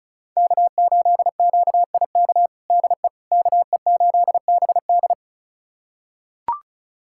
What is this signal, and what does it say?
Morse code, heard as a keyed tone, about 36 wpm, 700 Hz. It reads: K8QIK DE KE8BD
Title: USA Callsign DE Callsign - 5 Characters - 35wpm